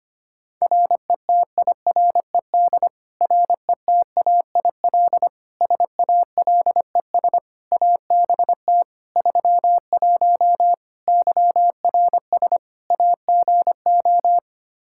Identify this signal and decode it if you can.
Morse code, heard as a keyed tone, about 25 wpm, 700 Hz. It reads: RETIRED RETAIL SALES ABT 31 YRS AGO
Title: Practice QSO - Exchange 3 - More Info - 25wpm